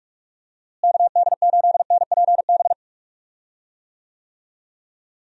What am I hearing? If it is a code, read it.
Morse code, heard as a keyed tone, about 46 wpm, 700 Hz. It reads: KD8NPB
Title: USA Call Signs - 6 Characters - Repeated 3x - 45wpm